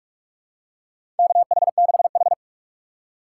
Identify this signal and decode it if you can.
Morse code, heard as a keyed tone, about 41 wpm, 700 Hz. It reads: KH6H
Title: USA Call Signs - 4 Characters - 4x Character Spacing - 45wpm